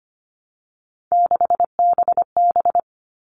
Morse code, heard as a keyed tone, about 25 wpm, 700 Hz. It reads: 6BB
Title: All Cut Numbers - 3-Digit 25wpm